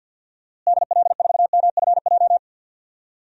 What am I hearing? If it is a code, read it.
Morse code, heard as a keyed tone, about 52 wpm, 700 Hz. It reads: DL4MFJ
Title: DXCC-100 Call Signs - 6 Characters - Repeated 3x - 50wpm